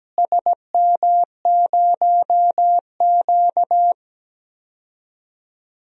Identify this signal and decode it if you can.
Morse code, heard as a keyed tone, about 17 wpm, 700 Hz. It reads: SM0Q
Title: DXCC-100 Call Signs - 4 Characters - Repeated 3x - 17wpm